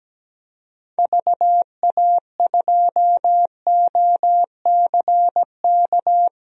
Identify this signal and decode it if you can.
Morse code, heard as a keyed tone, about 17 wpm, 700 Hz. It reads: VA2OCK
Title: DXCC-100 Call Signs - 6 Characters - 17wpm